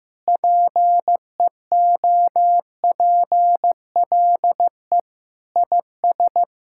Morse code, heard as a keyed tone, about 15 wpm, 700 Hz. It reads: PEOPLE IS
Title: Sentences from Top 1000 Words - Review Part 6 - 15wpm